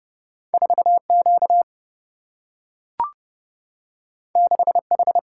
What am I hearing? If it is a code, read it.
Morse code, heard as a keyed tone, about 30 wpm, 700 Hz. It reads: 4 Q 6 H
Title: Single-Letter Single-Number 30wpm